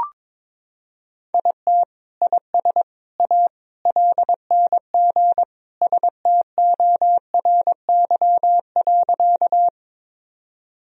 Morse code, heard as a keyed tone, about 22 wpm, 700 Hz. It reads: IT IS A LNG STORY.
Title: Sentences from Top 300 Words and Abbreviations 22wpm